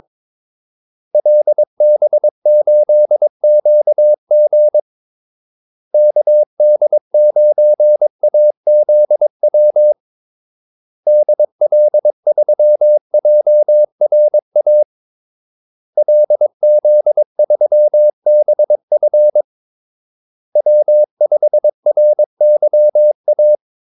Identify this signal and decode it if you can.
Morse code, heard as a keyed tone, about 22 wpm, 600 Hz. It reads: LB8QG KD9AZW DL3JRA LZ3BF W5RYA
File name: Callsigns-22wpm